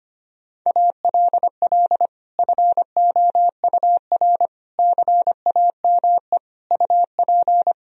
Morse code, heard as a keyed tone, about 25 wpm, 700 Hz. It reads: ALL FOUR CAME UP
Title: Sentences from Top 1000 Words - Review Part 1 - 25wpm